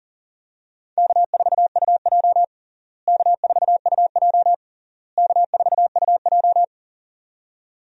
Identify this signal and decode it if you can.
Morse code, heard as a keyed tone, about 40 wpm, 700 Hz. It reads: K4UJ K4UJ K4UJ
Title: USA Call Signs - 4 Characters - Repeated 3x - 40wpm